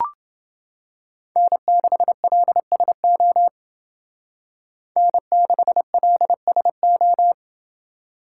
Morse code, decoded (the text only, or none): N6LSO N6LSO